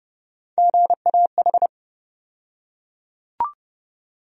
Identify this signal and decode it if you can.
Morse code, heard as a keyed tone, about 30 wpm, 700 Hz. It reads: GAH
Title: USA Call Sign Postfixes - 3 Letters 30wpm